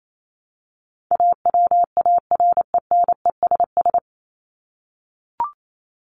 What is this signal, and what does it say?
Morse code, heard as a keyed tone, about 28 wpm, 700 Hz. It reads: AWARENESS
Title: Top 1501-1750 Words 28wpm